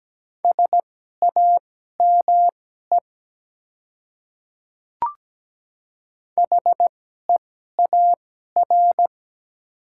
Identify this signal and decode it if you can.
Morse code, heard as a keyed tone, about 17 wpm, spaced as for 12 wpm, 700 Hz. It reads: SAME HEAR
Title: 2x Character Spacing - Top 300 Words - 17wpm